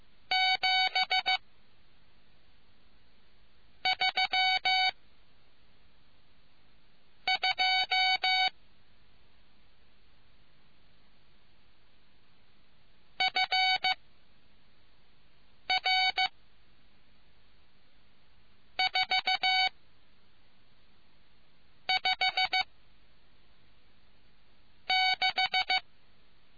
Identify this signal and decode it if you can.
Morse code, heard as a keyed tone, about 15 wpm, spaced as for 4 wpm, 750 Hz. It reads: 7 3 2 FR 4 5 6